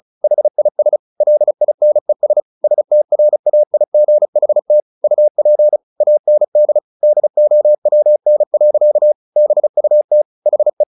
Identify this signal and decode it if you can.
Morse code, heard as a keyed tone, about 34 wpm, 600 Hz. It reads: HIS LINES STRAIGHT UP AND DOWN. BUT HE
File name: the old man and the sea / the old man and the sea 1 35wpm